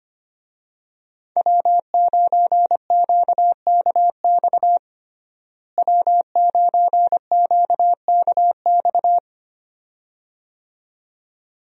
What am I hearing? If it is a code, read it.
Morse code, heard as a keyed tone, about 25 wpm, 700 Hz. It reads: W9QKX W9QKX
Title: USA Call Signs - 5 Characters - Repeated 2x - 25wpm